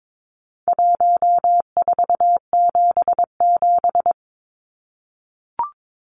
Morse code, heard as a keyed tone, about 22 wpm, 700 Hz. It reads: 1477
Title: Numbers - Contest Style - 4-Digit 22wpm